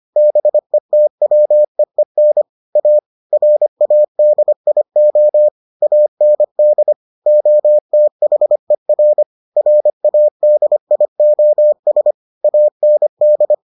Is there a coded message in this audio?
Yes: Morse code from 0.2 to 13.5 s, heard as a keyed tone, about 25 wpm, 600 Hz. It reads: BETWEEN A RADIO AND OTHER RADIOS AND